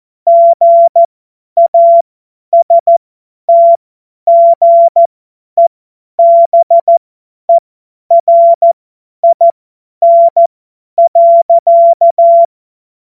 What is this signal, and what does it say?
Morse code, heard as a keyed tone, about 14 wpm, spaced as for 10 wpm, 680 Hz. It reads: GASTGEBERIN.
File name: Morsekurs / Witze / Witz 25. Januar 2022, 50 F70